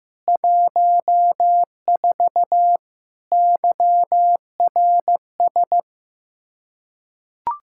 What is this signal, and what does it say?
Morse code, heard as a keyed tone, about 15 wpm, 700 Hz. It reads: 14 YRS
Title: K1USN SST QSO Practice 15wpm